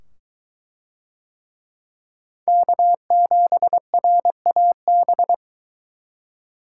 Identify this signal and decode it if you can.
Morse code, heard as a keyed tone, about 22 wpm, 700 Hz. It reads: K7RAB